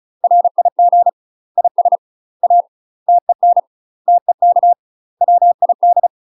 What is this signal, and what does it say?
Morse code, heard as a keyed tone, about 34 wpm, 700 Hz. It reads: RIG IS A TEN TEK WID
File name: UK MASTER QSOs TRAINING / 35WPM / UK MASTER QSOs 0025